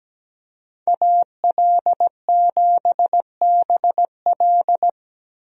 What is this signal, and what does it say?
Morse code, heard as a keyed tone, about 17 wpm, 700 Hz. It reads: AL7BL
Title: USA Call Signs - 5 Characters - 17wpm